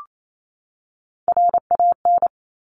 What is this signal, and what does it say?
Morse code, heard as a keyed tone, about 28 wpm, 700 Hz. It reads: RAN
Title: Top Three Letter Combinations in Words 28wpm